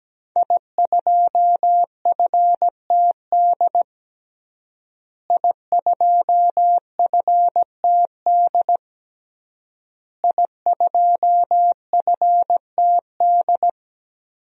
Morse code, heard as a keyed tone, about 17 wpm, 700 Hz. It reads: I2FTD I2FTD I2FTD
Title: DXCC-100 Call Signs - 5 Characters - Repeated 3x - 17wpm